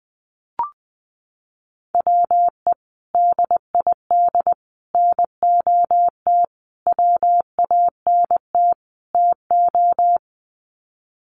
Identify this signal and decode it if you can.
Morse code, heard as a keyed tone, about 20 wpm, 700 Hz. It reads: WE DID NOT WANT TO
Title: Sets of 5 Words Encore 20wpm